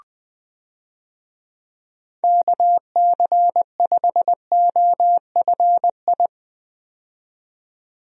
Morse code, heard as a keyed tone, about 20 wpm, 700 Hz. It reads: KC5OFI
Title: USA Call Signs - 6 Characters - 20wpm